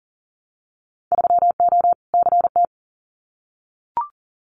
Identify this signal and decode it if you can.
Morse code, heard as a keyed tone, about 40 wpm, 700 Hz. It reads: 3O CT
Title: Winter Field Day Exchanges 40wpm